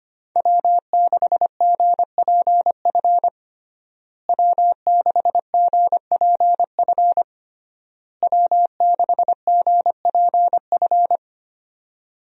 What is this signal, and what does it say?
Morse code, heard as a keyed tone, about 25 wpm, 700 Hz. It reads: W6GPF W6GPF W6GPF
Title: USA Call Signs - 5 Characters - Repeated 3x - 25wpm